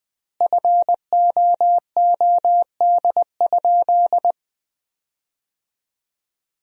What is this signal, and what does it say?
Morse code, heard as a keyed tone, about 21 wpm, 700 Hz. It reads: FOOD?
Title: 3x Word-Spacing - Sentences from Top 300 Words - 20wpm